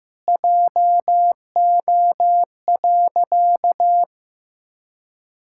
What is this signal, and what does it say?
Morse code, heard as a keyed tone, about 16 wpm, 700 Hz. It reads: JO.
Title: Pangram Challenge v2 - 15wpm